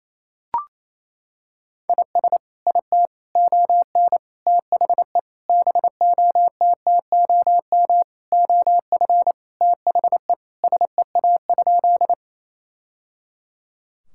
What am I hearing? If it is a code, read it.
Morse code, heard as a keyed tone, about 28 wpm, 700 Hz. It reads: IS IT ON THE BOTTOM OF THE SEA?
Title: Sentences from Top 1000 Words - Review Part 6 - 28wpm